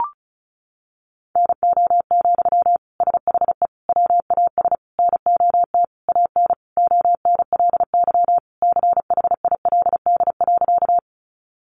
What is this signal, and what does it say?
Morse code, heard as a keyed tone, about 34 wpm, 700 Hz. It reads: NO, SHE WAS NOT AN ONLY CHILD.
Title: Sentences from Top 800 Words 35wpm